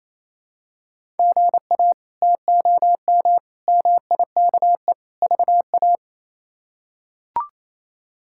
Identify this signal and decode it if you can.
Morse code, heard as a keyed tone, about 28 wpm, 700 Hz. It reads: GA TOM MIKE VA
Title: K1USN SST QSO Practice - 28wpm